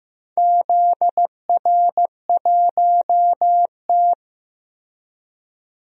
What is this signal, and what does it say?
Morse code, heard as a keyed tone, about 15 wpm, 700 Hz. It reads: ZR1T
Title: DXCC-100 Call Signs - 4 Characters - Repeated 3x - 15wpm